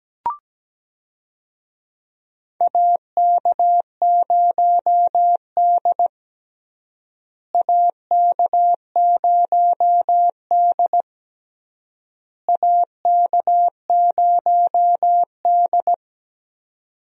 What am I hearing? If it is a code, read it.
Morse code, heard as a keyed tone, about 17 wpm, 700 Hz. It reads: AK0D AK0D AK0D
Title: USA Call Signs - 4 Characters - Repeated 3x - 17wpm